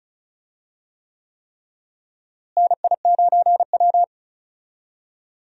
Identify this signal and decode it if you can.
Morse code, heard as a keyed tone, about 36 wpm, 700 Hz. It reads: NI9W
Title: USA Call Signs - 4 Characters - Repeated 3x - 35wpm